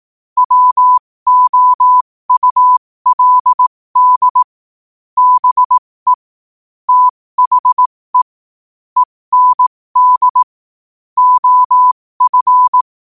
Morse code, heard as a keyed tone, about 18 wpm, spaced as for 15 wpm, 1000 Hz. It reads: WOULD BE THE END OF
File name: qotdfiles / 15wpm